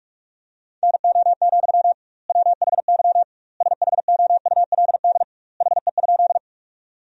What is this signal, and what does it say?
Morse code, heard as a keyed tone, about 46 wpm, 700 Hz. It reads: NO, WHY SHOULD HE?
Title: 2x Word-Spacing - Sentences from Top 300 Words - 45wpm